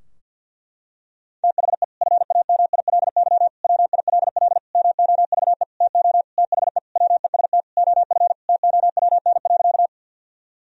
Morse code, heard as a keyed tone, about 50 wpm, 700 Hz. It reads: THE FAMILY WILL MOVE TO THE WEST OF TOWN.
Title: Sentences from Top 1000 Words - Review Part 2 - 50wpm